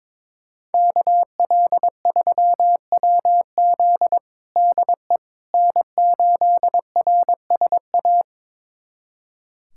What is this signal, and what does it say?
Morse code, heard as a keyed tone, about 21 wpm, 700 Hz. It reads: KL3WZ DE N8RSA
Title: USA Callsign DE Callsign - 5 Characters - 22wpm